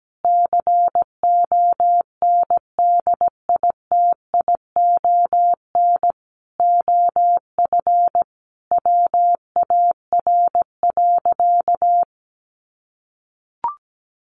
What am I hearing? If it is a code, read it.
Morse code, heard as a keyed tone, about 17 wpm, 700 Hz. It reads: CONDITION OF WAR.
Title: Sentences from Top 1500 Words 17wpm